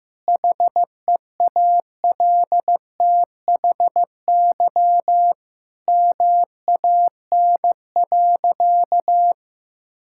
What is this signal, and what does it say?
Morse code, heard as a keyed tone, about 15 wpm, 700 Hz. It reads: HEALTHY MAN.
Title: Sentences from Top 1000 Words - Review Part 5 - 15wpm